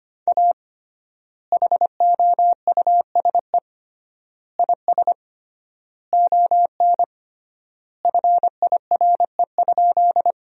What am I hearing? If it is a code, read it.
Morse code, heard as a keyed tone, about 25 wpm, 700 Hz. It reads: A HOUSE IS ON FIRE?
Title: 3x Word-Spacing - Sentences from Top 400 Words - 25wpm